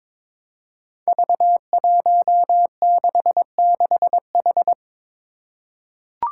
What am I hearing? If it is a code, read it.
Morse code, heard as a keyed tone, about 22 wpm, 700 Hz. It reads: V166H
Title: Five Challenging Letter-Number Combinations 22wpm